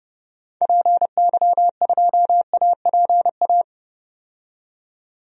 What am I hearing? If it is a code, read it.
Morse code, heard as a keyed tone, about 32 wpm, 700 Hz. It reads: PY2APA
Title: DXCC-100 Call Signs - 6 Characters - 30wpm